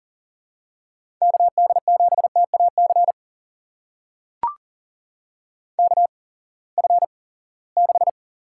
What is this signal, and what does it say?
Morse code, heard as a keyed tone, about 42 wpm, spaced as for 23 wpm, 700 Hz. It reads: KD7TAC KF6
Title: USA Call Signs - 6 Characters - 8x Character Spacing - 40wpm